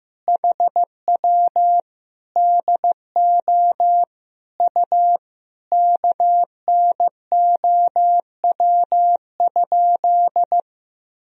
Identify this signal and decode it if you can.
Morse code, heard as a keyed tone, about 16 wpm, 700 Hz. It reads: HW DO U KNOW?
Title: Sentences from Top 100 Words and Abbreviations 15wpm